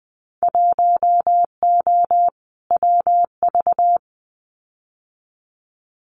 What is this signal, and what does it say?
Morse code, heard as a keyed tone, about 20 wpm, 700 Hz. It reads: 1O WV
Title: Winter Field Day Exchanges Encore 20wpm